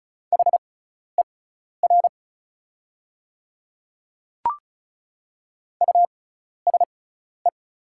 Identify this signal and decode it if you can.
Morse code, heard as a keyed tone, about 34 wpm, spaced as for 12 wpm, 690 Hz. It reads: HER USE
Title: 6x Character Spacing - Top 100 Words - 35wpm